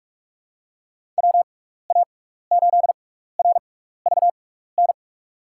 Morse code, heard as a keyed tone, about 46 wpm, spaced as for 15 wpm, 700 Hz. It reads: WA8RVN
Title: USA Call Signs - 6x Character Spacing - 45wpm